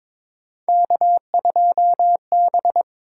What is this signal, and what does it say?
Morse code, heard as a keyed tone, about 22 wpm, 700 Hz. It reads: K2B
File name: practice / 22wpm / Three Letter-Number Combinations - 22wpm